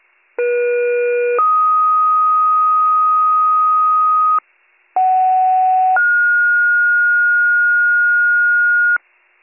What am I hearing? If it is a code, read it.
Two-tone page, heard as a one-second tone, then a three-second tone, about 480 then 1250 Hz, about 750 then 1430 Hz, with noise. It is heard from 0.4 to 9.0 s.